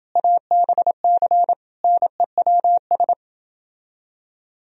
Morse code, heard as a keyed tone, about 28 wpm, 700 Hz. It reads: ABC NEWS